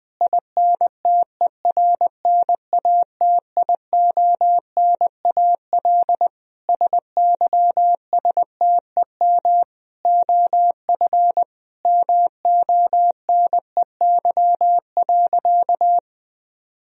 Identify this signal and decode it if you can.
Morse code, heard as a keyed tone, about 21 wpm, 700 Hz. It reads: INTERNATIONAL SYSTEM OF MONEY.
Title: Sentences from Top 1000 Words - Review Part 4 - 20wpm